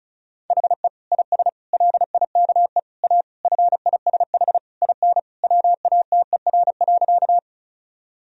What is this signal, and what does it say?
Morse code, heard as a keyed tone, about 36 wpm, 700 Hz. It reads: HE IS LIKE A FISH IN WATER.
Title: Sentences from Top 1000 Words - Review Part 1 - 35wpm